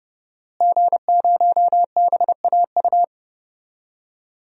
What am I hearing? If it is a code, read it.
Morse code, heard as a keyed tone, about 32 wpm, 700 Hz. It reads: G0BAU
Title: DXCC-100 Call Signs - 5 Characters - Repeated 3x - 30wpm